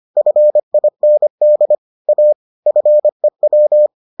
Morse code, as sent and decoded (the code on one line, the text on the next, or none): ..-. .. -. -.. / .- / ..-. .--
FIND A FEW